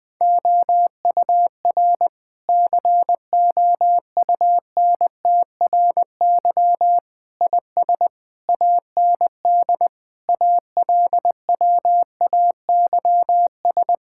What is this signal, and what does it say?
Morse code, heard as a keyed tone, about 20 wpm, 700 Hz. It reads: OUR COUNTRY IS AND ALWAYS
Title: Sentences from Top 1000 Words - Review Part 6 - 20wpm